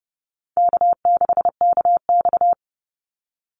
Morse code, heard as a keyed tone, about 30 wpm, 700 Hz. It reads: K6KX
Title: Call Signs Repeated 2x - 30wpm